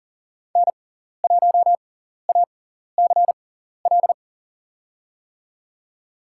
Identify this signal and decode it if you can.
Morse code, heard as a keyed tone, about 40 wpm, spaced as for 14 wpm, 700 Hz. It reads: N1ACL
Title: USA Call Signs - 5 Characters - 6x Character Spacing - 40wpm